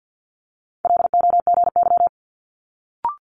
Morse code, heard as a keyed tone, about 50 wpm, 700 Hz. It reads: LOGY
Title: Multi-letter Phonemes Quick Fire 50wpm